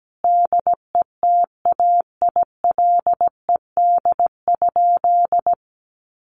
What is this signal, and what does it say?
Morse code, heard as a keyed tone, about 17 wpm, 700 Hz. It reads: DETAILED?
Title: Sentences from Top 1500 Words 17wpm